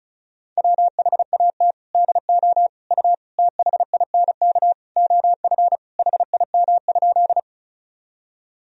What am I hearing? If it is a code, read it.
Morse code, heard as a keyed tone, about 36 wpm, 700 Hz. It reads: WHAT DO U THINK OF HIM?
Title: Sentences from Top 200 Words and Abbreviations 35wpm